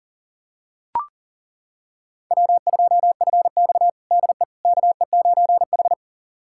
Morse code, heard as a keyed tone, about 37 wpm, 700 Hz. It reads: W2FX DE KE9H
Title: USA Callsign DE Callsign - 4 Characters - 40wpm